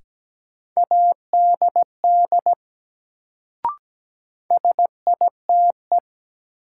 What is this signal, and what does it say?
Morse code, heard as a keyed tone, about 17 wpm, 700 Hz. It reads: ADD SITE
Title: Sentences from Top 1000 Words - Review Part 1 - New Word Drill - 17wpm